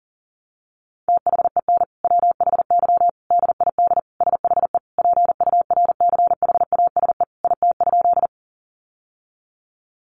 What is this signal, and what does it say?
Morse code, heard as a keyed tone, about 40 wpm, 700 Hz. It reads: THEN WHY DID SHE PURCHASE IT?
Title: Sentences from Top 800 Words 40wpm